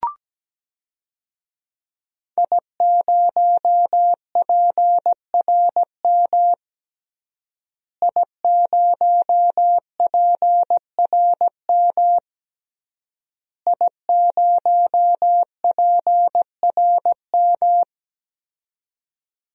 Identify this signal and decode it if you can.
Morse code, heard as a keyed tone, about 17 wpm, 700 Hz. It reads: I0PRM I0PRM I0PRM
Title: DXCC-100 Call Signs - 5 Characters - Repeated 3x - 17wpm